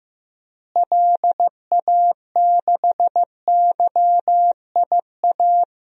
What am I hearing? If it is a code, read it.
Morse code, heard as a keyed tone, about 15 wpm, 700 Hz. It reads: LA6YIA
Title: DXCC-100 Call Signs - 6 Characters - Repeated 3x - 15wpm